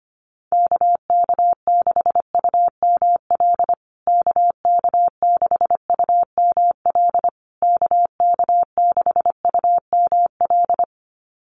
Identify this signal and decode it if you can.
Morse code, heard as a keyed tone, about 25 wpm, 700 Hz. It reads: KK6UML KK6UML KK6UML
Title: USA Call Signs Repeated 3x 25wpm